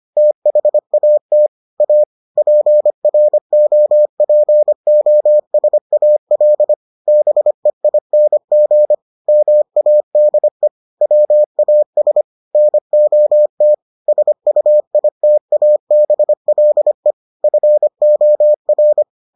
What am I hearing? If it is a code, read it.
Morse code, heard as a keyed tone, about 25 wpm, 600 Hz. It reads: THAT A PROPOSAL BEING MADE WAS NOT SUITABLE FOR